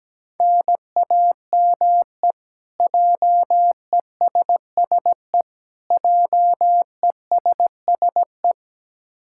Morse code, heard as a keyed tone, about 17 wpm, 700 Hz. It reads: NAME JESSE JESSE
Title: K1USN SST QSO Practice 17wpm